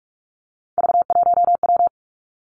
Morse code, heard as a keyed tone, about 46 wpm, 700 Hz. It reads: V1W
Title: Three Challenging Letter-Number Combinations 45wpm